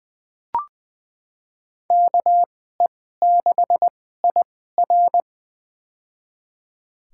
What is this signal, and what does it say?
Morse code, heard as a keyed tone, about 21 wpm, spaced as for 15 wpm, 700 Hz. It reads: KE6IR
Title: USA Call Signs - 5 Characters - 2x Character Spacing - 20wpm